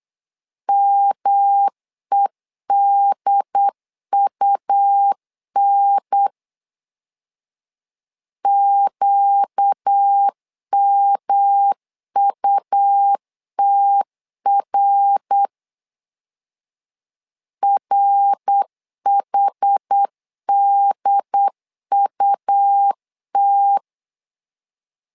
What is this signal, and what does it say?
Morse code, heard as a keyed tone, about 8 wpm, 800 Hz. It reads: MEDUN QMUTR RHDUT
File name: 4 Q 8wpm